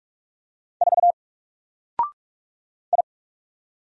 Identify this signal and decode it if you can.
Morse code, heard as a keyed tone, about 46 wpm, 700 Hz. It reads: 4 I
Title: Single Character (Letter:Number:Punctuation:Prosign) Alt 45wpm